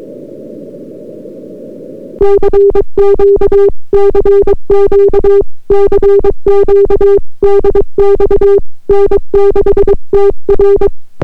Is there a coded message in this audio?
Yes: Morse code from 2.2 to 10.9 s, heard as a keyed tone, about 22 wpm, 400 Hz. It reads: CQ CQ CQ DX N6TR